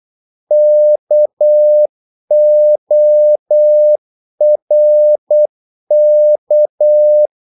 Morse code, heard as a keyed tone, about 8 wpm, 600 Hz. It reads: KORK
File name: Korte ord DK 8wpm